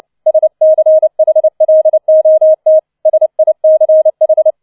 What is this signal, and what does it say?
Morse code, heard as a keyed tone, about 29 wpm, 630 Hz. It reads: SCHLOT SICH
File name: 006 Sechster Streich 30wpm